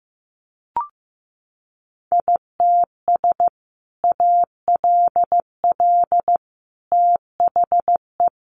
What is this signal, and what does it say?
Morse code, heard as a keyed tone, about 15 wpm, 700 Hz. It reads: ITS ALL THE
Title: Sentences from Top 600 Words 15wpm